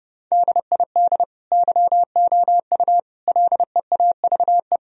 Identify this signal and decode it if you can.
Morse code, heard as a keyed tone, about 30 wpm, 700 Hz. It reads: DID YOU LEAVE
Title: Sentences from Top 1000 Words - Review Part 3 - 30wpm